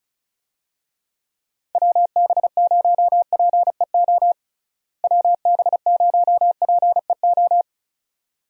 Morse code, heard as a keyed tone, about 36 wpm, 700 Hz. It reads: WB0PEO WB0PEO
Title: USA Call Signs - 6 Characters - Repeated 2x - 35wpm